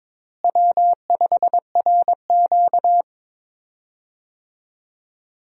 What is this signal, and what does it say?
Morse code, heard as a keyed tone, about 22 wpm, 700 Hz. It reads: W5RQ
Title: USA Call Signs - 4 Characters - Repeated 2x - 22wpm